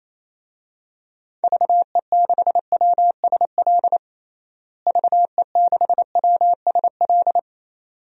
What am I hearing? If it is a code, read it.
Morse code, heard as a keyed tone, about 28 wpm, 700 Hz. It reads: VE6WSL VE6WSL
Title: DXCC-100 Call Signs - 6 Characters - Repeated 2x - 28wpm